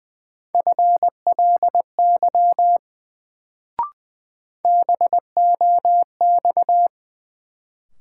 Morse code, heard as a keyed tone, about 20 wpm, 700 Hz. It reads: FLY BOX
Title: Sentences from Top 500 Words - New Word Drill - 20wpm